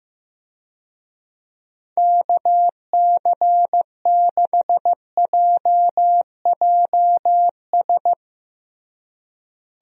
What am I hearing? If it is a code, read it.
Morse code, heard as a keyed tone, about 15 wpm, 700 Hz. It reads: KC6JJS
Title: USA Call Signs - 6 Characters - Repeated 2x - 15wpm